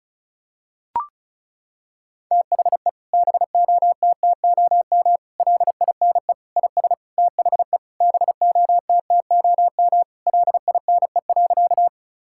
Morse code, heard as a keyed tone, about 36 wpm, 700 Hz. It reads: THE BOTTOM LINE IS THE BOTTOM LINE.
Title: Sentences from Top 1000 Words - Review Part 6 - 35wpm